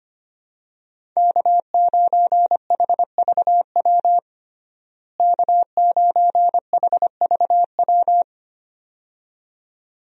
Morse code, heard as a keyed tone, about 25 wpm, 700 Hz. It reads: K9HVW K9HVW
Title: USA Call Signs - 5 Characters - Repeated 2x - 25wpm